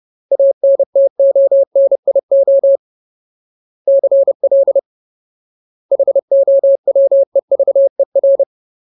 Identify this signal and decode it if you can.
Morse code, heard as a keyed tone, about 30 wpm, 550 Hz. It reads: ANTONIO CL HOWEVER
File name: Ultimate Word List / 30WPM Ultimate